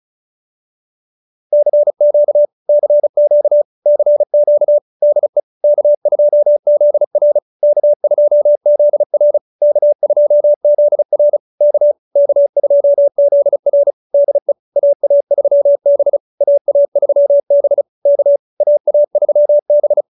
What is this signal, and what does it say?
Morse code, heard as a keyed tone, about 34 wpm, 590 Hz. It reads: CQ CQ CQ DE K2ZR K2ZR K2ZR K K2ZR DE AA3B AA3B K AA3B